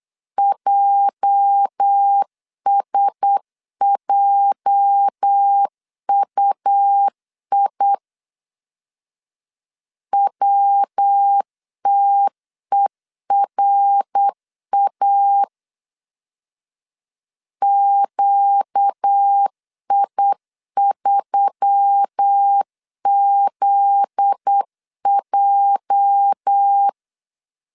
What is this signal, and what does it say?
Morse code, heard as a keyed tone, about 8 wpm, 800 Hz. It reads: JSJUI WTERA QI3ZJ
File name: LEZIONI / 8 YJ45 / 8 YJ 8wpm 3313